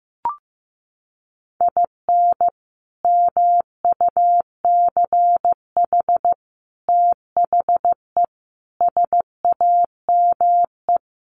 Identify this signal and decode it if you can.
Morse code, heard as a keyed tone, about 15 wpm, 700 Hz. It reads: IN MUCH THE SAME
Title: Sets of 5 Words Encore 15wpm